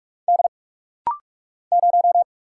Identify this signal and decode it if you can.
Morse code, heard as a keyed tone, about 46 wpm, 700 Hz. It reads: D 0
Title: Single Letter-Number Alt - Rapid-Fire 45wpm